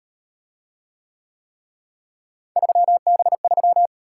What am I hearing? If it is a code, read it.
Morse code, heard as a keyed tone, about 38 wpm, 700 Hz. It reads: 3B3